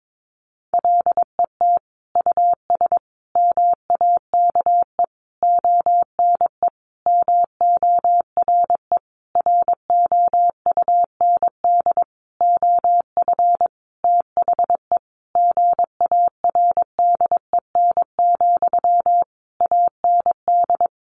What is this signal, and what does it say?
Morse code, heard as a keyed tone, about 22 wpm, 700 Hz. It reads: LET US MAKE ONE MORE ROUND OF THE GARDEN, AND
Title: Sentences from Top 700 Words 22wpm